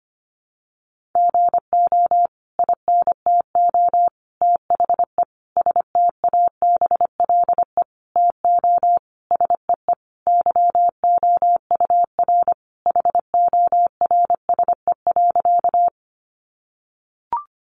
Morse code, heard as a keyed tone, about 25 wpm, 700 Hz. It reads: GO INTO THE STABLE TO SEE YOUR HORSE.
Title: Sentences from Top 1500 Words 25wpm